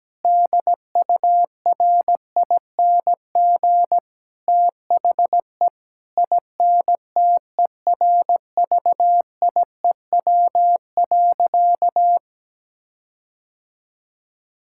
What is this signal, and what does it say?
Morse code, heard as a keyed tone, about 17 wpm, 700 Hz. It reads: DURING THE INTERVIEW.
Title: Sentences from Top 1600 Words 17wpm